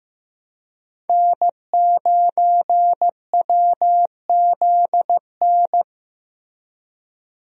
Morse code, heard as a keyed tone, about 15 wpm, 700 Hz. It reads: N9WZN
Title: USA Call Signs - 5 Characters - Repeated 3x - 15wpm